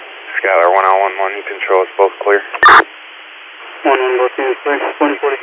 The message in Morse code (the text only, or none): none